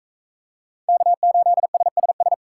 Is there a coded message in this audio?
Yes: Morse code from 0.9 to 2.3 s, heard as a keyed tone, about 42 wpm, 700 Hz. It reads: K8SSS